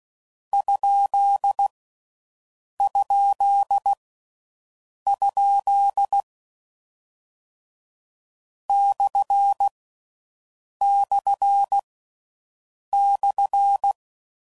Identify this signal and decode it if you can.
Morse code, heard as a keyed tone, about 16 wpm, 1350 Hz. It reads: ? ? ? / / /
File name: Mixed / Misc / Punct